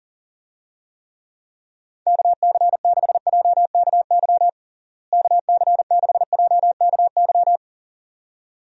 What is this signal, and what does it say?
Morse code, heard as a keyed tone, about 37 wpm, 700 Hz. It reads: KC6JKY KC6JKY
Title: USA Call Signs - 6 Characters - Repeated 2x - 40wpm